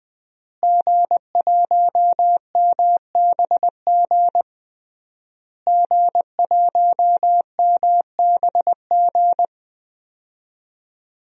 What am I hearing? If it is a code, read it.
Morse code, heard as a keyed tone, about 20 wpm, 700 Hz. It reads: G1MBG G1MBG
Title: DXCC-100 Call Signs - 5 Characters - Repeated 3x - 20wpm